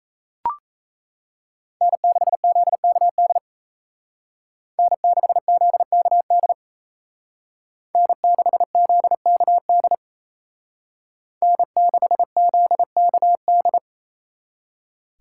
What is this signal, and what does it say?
Morse code, heard as a keyed tone, about 35 wpm, 700 Hz. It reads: N6ZKD N6ZKD N6ZKD N6ZKD